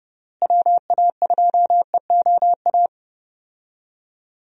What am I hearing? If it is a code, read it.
Morse code, heard as a keyed tone, about 30 wpm, 700 Hz. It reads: WA2EOA
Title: USA Call Signs - 6 Characters - Repeated 3x - 30wpm